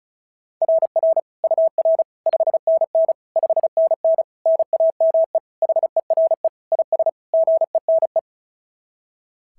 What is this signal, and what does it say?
Morse code, heard as a keyed tone, about 36 wpm, 690 Hz. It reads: RR UR 5NN 5NN NAME HERE IS GENE
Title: SOTA QSO Practice 35wpm